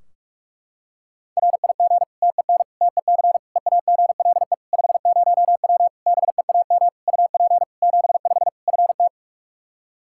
Morse code, heard as a keyed tone, about 46 wpm, 700 Hz. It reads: RIG TEN TEC EAGLE 50W BEAM UP 75 FT
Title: Practice QSO - Exchange 2 - Info - 45wpm